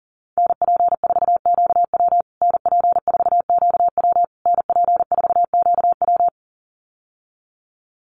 Morse code, heard as a keyed tone, about 37 wpm, 700 Hz. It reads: NP4QW NP4QW NP4QW
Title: USA Call Signs Repeated 3x 40wpm